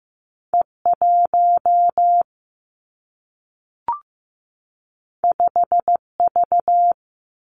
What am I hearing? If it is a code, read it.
Morse code, heard as a keyed tone, about 15 wpm, 700 Hz. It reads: E 1 5 V
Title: Two Challenging Letter-Number Combinations 15wpm